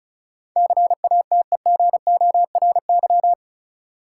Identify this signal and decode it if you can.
Morse code, heard as a keyed tone, about 34 wpm, 700 Hz. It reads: CATEGORY
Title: Sentences from Top 1000 Words - Review Part 7 - New Word Drill - 35wpm